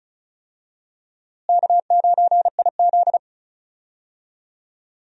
Morse code, heard as a keyed tone, about 36 wpm, 700 Hz. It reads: K9IZ
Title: USA Call Signs - 4 Characters - Repeated 2x - 35wpm